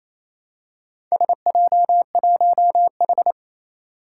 Morse code, heard as a keyed tone, about 29 wpm, 700 Hz. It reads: SJ1H
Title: Four Challenging Letter-Number Combinations 28wpm